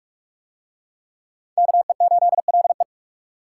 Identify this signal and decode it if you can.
Morse code, heard as a keyed tone, about 40 wpm, 700 Hz. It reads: KE8LE